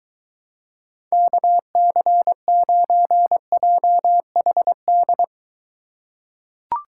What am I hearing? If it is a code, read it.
Morse code, heard as a keyed tone, about 23 wpm, 700 Hz. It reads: KC9JHD